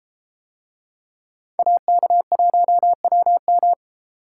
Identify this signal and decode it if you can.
Morse code, heard as a keyed tone, about 30 wpm, 700 Hz. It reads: AK1WM